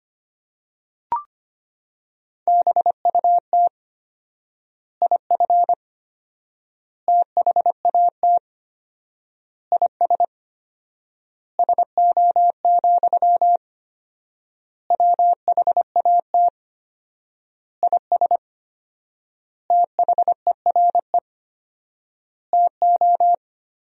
Morse code, heard as a keyed tone, about 25 wpm, 700 Hz. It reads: BUT IF THAT IS SO, WHAT IS THERE TO
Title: 4x Word-Spacing - Sentences from Top 200 Words - 25wpm